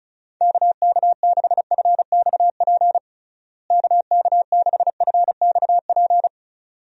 Morse code, heard as a keyed tone, about 34 wpm, 700 Hz. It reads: KK6FXP KK6FXP
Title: USA Call Signs - 6 Characters - Repeated 2x - 35wpm